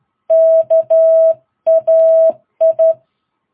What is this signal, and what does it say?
Morse code, heard as a keyed tone, about 13 wpm, 660 Hz. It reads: KAI
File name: jh8cbh / 2022cw